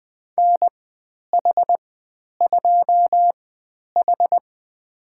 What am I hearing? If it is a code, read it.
Morse code, heard as a keyed tone, about 20 wpm, spaced as for 10 wpm, 700 Hz. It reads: NH2H
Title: 109 Lesson 18g - Letter m - Call Signs 20wpm